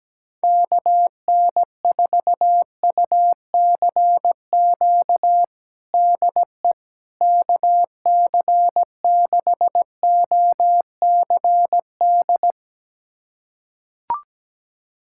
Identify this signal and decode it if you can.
Morse code, heard as a keyed tone, about 17 wpm, 700 Hz. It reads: KN4UCQ DE KC6OCD
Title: USA Callsign DE Callsign - 6 Characters - 17wpm